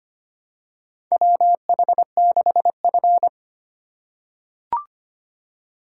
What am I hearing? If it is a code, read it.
Morse code, heard as a keyed tone, about 25 wpm, 700 Hz. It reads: WH6F
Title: USA Call Signs - 4x Character Spacing - 25wpm